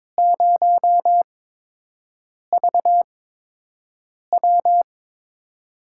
Morse code, heard as a keyed tone, about 22 wpm, spaced as for 6 wpm, 700 Hz. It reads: 0VW
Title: USA Call Signs - 4 Characters - 8x Character Spacing - 22wpm